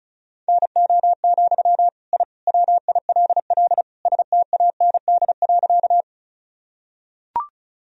Morse code, heard as a keyed tone, about 36 wpm, 700 Hz. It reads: NO, I WILL STAND.
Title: Sentences from Top 1000 Words - Review Part 1 - 35wpm